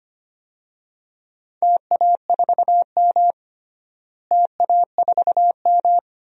Morse code, heard as a keyed tone, about 25 wpm, 700 Hz. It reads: TA4M TA4M
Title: DXCC-100 Call Signs - 4 Characters - Repeated 3x - 25wpm